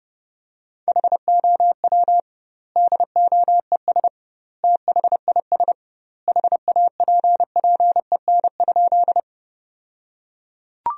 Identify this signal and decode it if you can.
Morse code, heard as a keyed tone, about 32 wpm, 700 Hz. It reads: HOW DOES THIS HAPPEN?
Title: 2x Word-Spacing - Sentences from Top 400 Words - 30wpm